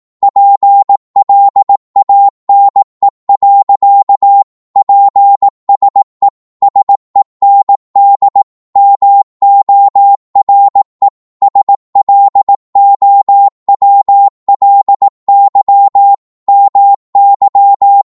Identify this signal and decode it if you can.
Morse code, heard as a keyed tone, about 18 wpm, 800 Hz. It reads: PLANE. PSE SEND MORE SLOWLY MY
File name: Morse / qso-18wpm